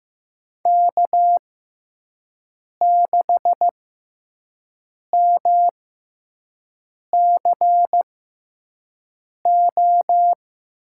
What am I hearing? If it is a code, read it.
Morse code, heard as a keyed tone, about 15 wpm, spaced as for 5 wpm, 700 Hz. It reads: K6MCO
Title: USA Call Signs - 5 Characters - 6x Character Spacing - 15wpm